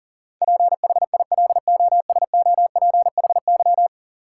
Morse code, heard as a keyed tone, about 40 wpm, 700 Hz. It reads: PHILOSOPHY
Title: Sentences from Top 900 Words - New Word Drill - 40wpm